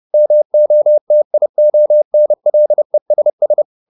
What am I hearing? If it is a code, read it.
Morse code, heard as a keyed tone, about 30 wpm, 600 Hz. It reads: MOTIONLESS